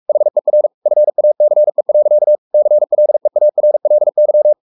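Morse code, heard as a keyed tone, about 44 wpm, 600 Hz. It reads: HER FACE. CLEARLY
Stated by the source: Morse code at 43 wpm sending Ken Follett's novel Never in English, from CW med Gnister.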